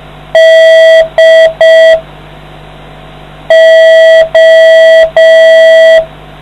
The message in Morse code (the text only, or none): DO